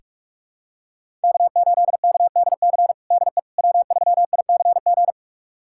Morse code, heard as a keyed tone, about 46 wpm, 700 Hz. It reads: K8KDC DE W3ICG
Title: USA Callsign DE Callsign - 5 Characters - 45wpm